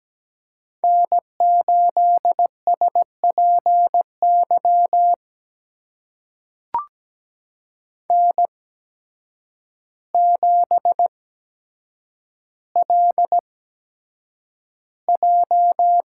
Morse code, heard as a keyed tone, about 17 wpm, spaced as for 8 wpm, 700 Hz. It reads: N8SPY N7LJ
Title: USA Call Signs - 5 Characters - 8x Character Spacing - 17wpm